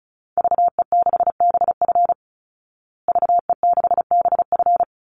Morse code, heard as a keyed tone, about 34 wpm, 700 Hz. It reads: VE6BF VE6BF
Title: Call Signs Repeated 2x - 35wpm